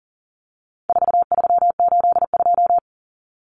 Morse code, heard as a keyed tone, about 40 wpm, 700 Hz. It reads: 4382
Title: Common Cut Numbers - 4-Digit 40wpm